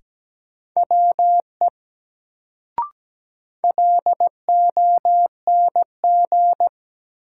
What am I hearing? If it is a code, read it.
Morse code, heard as a keyed tone, about 17 wpm, 700 Hz. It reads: WE LONG
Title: Sentences from Top 100 Words - New Word Drill - 17wpm